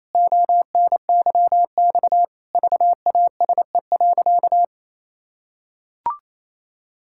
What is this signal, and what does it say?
Morse code, heard as a keyed tone, about 28 wpm, 700 Hz. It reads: ONYX VASE.
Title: Pangram Challenge v2 - 28wpm